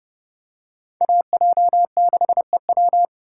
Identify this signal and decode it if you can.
Morse code, heard as a keyed tone, about 26 wpm, 700 Hz. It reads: AJ6EW